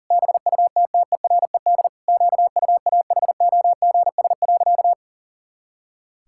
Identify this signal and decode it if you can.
Morse code, heard as a keyed tone, about 40 wpm, 700 Hz. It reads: BUTTERED QUAHOGS.
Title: Pangram Challenge v2 - 40wpm